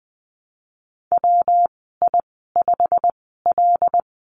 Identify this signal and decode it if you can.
Morse code, heard as a keyed tone, about 20 wpm, spaced as for 14 wpm, 700 Hz. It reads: WI5L
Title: 154 Lesson 25g - Letter p - Call Signs 20wpm